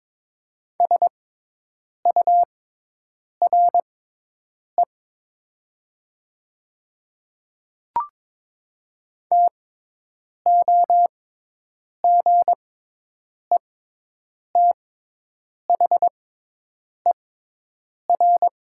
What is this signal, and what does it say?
Morse code, heard as a keyed tone, about 22 wpm, spaced as for 8 wpm, 700 Hz. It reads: SURE TOGETHER
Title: 6x Character Spacing - Top 300 Words - 22wpm